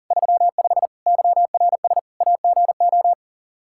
Morse code, heard as a keyed tone, about 37 wpm, 700 Hz. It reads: 35 YRS AGO
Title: Practice QSO - Exchange 3 - More Info - 40wpm